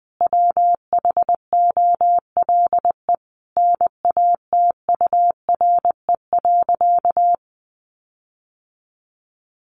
Morse code, heard as a keyed tone, about 20 wpm, 700 Hz. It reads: WHOLE NATURE.
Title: Sentences from Top 1500 Words 20wpm